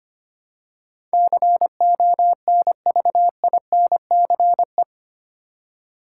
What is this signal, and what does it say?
Morse code, heard as a keyed tone, about 25 wpm, 700 Hz. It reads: CONVINCE
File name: Eight Letter Words - 25wpm